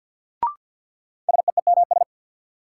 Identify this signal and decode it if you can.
Morse code, heard as a keyed tone, about 52 wpm, 700 Hz. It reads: SEEMS
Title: Sets of 3 Words - New Word Drill - 50wpm